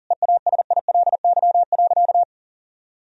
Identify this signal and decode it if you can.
Morse code, heard as a keyed tone, about 37 wpm, 700 Hz. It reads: EASILY.
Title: Sentences from Top 1000 Words - Review Part 3 - 40wpm